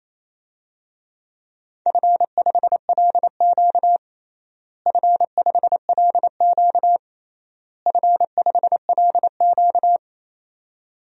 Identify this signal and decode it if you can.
Morse code, heard as a keyed tone, about 27 wpm, 700 Hz. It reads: F5LQ F5LQ F5LQ
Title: DXCC-100 Call Signs - 4 Characters - Repeated 3x - 28wpm